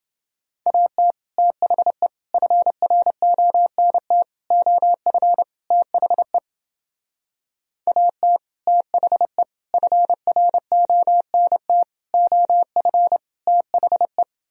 Morse code, heard as a keyed tone, about 30 wpm, 700 Hz. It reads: AT THE FRONT OF THE AT THE FRONT OF THE